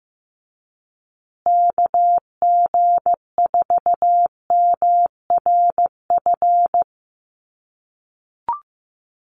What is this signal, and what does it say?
Morse code, heard as a keyed tone, about 15 wpm, 700 Hz. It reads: KG4MRF
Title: USA Call Signs Repeated 2x 15wpm